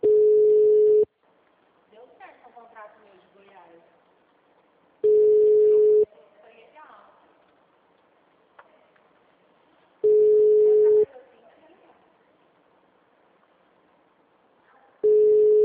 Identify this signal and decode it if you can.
Ringback tone (Europe, 425 Hz), with speech.
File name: out- -2012- - - .1348